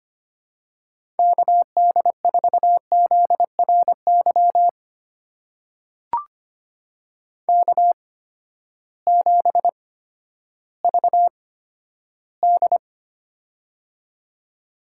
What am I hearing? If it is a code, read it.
Morse code, heard as a keyed tone, about 25 wpm, spaced as for 12 wpm, 700 Hz. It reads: KD4ZRY K7VD
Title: USA Call Signs - 8x Character Spacing - 25wpm